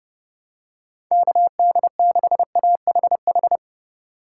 Morse code, heard as a keyed tone, about 30 wpm, 700 Hz. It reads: KD6AHH
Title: USA Call Signs - 2x Character Spacing - 30wpm